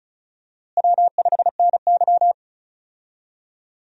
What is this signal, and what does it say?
Morse code, heard as a keyed tone, about 36 wpm, 700 Hz. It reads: W5NY
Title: USA Call Signs - 4 Characters - Repeated 3x - 35wpm